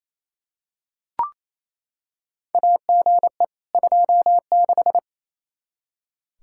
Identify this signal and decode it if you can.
Morse code, heard as a keyed tone, about 28 wpm, 700 Hz. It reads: AGE 26
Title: Practice QSO - Exchange 3 - More Info - 28wpm